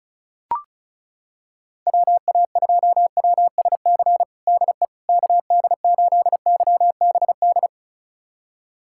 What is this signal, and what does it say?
Morse code, heard as a keyed tone, about 34 wpm, 700 Hz. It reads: WA2WSC DE KD8YBD
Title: USA Callsign DE Callsign - 6 Characters - 35wpm